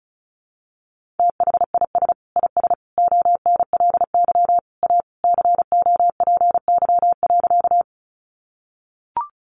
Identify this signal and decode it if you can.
Morse code, heard as a keyed tone, about 34 wpm, 700 Hz. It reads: THIS IS ONLY A COPY.
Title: Sentences from Top 1400 Words 35wpm